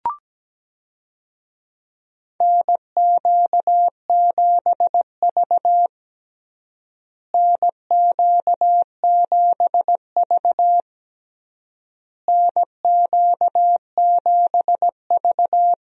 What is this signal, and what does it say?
Morse code, heard as a keyed tone, about 17 wpm, 700 Hz. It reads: NQ7V NQ7V NQ7V
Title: USA Call Signs - 4 Characters - Repeated 3x - 17wpm